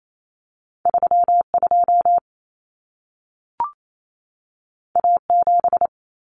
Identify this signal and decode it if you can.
Morse code, heard as a keyed tone, about 28 wpm, 700 Hz. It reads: 32 A7
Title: Common Cut Numbers - 2-Digit 28wpm